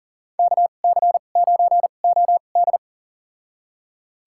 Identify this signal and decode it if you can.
Morse code, heard as a keyed tone, about 40 wpm, spaced as for 29 wpm, 700 Hz. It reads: KC9OD
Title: USA Call Signs - 5 Characters - 2x Character Spacing - 40wpm